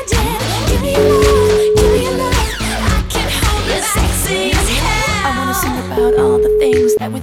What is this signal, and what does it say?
Ringback tone (Europe, 425 Hz), with noise.